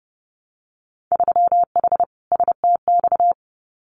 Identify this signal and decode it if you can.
Morse code, heard as a keyed tone, about 30 wpm, 700 Hz. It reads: 3H STX
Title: Winter Field Day Exchanges 30wpm